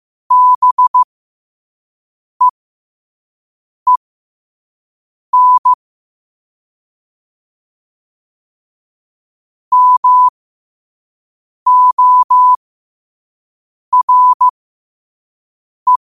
Morse code, heard as a keyed tone, about 15 wpm, spaced as for 5 wpm, 1000 Hz. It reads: BEEN MORE
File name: qotdfiles / 5wpm